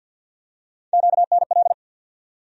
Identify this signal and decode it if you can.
Morse code, heard as a keyed tone, about 51 wpm, 700 Hz. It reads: QNL
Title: USA Call Sign Postfixes - 3 Letters 50wpm